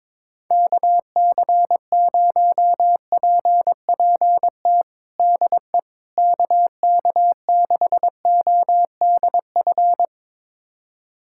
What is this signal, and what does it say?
Morse code, heard as a keyed tone, about 22 wpm, 700 Hz. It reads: KC0PPT DE KK6ODF
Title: USA Callsign DE Callsign 22wpm